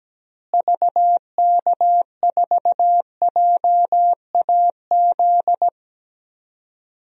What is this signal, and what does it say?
Morse code, heard as a keyed tone, about 17 wpm, 700 Hz. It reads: VK4JAZ
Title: DXCC-100 Call Signs - 6 Characters - Repeated 3x - 17wpm